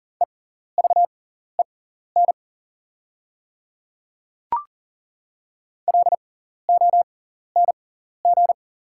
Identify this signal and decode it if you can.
Morse code, heard as a keyed tone, about 39 wpm, spaced as for 14 wpm, 690 Hz. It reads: EVEN LONG
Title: 6x Character Spacing - Top 300 Words - 40wpm